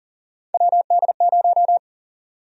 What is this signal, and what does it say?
Morse code, heard as a keyed tone, about 40 wpm, 700 Hz. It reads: WD0
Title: USA Call Sign Prefixes - 40wpm